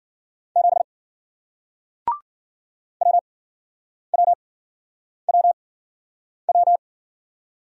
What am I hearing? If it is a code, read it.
Morse code, heard as a keyed tone, about 53 wpm, spaced as for 9 wpm, 700 Hz. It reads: 7 WWWW